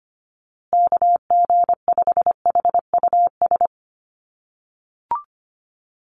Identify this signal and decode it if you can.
Morse code, heard as a keyed tone, about 25 wpm, 700 Hz. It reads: KG5HUS
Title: USA Call Signs Repeated 3x 25wpm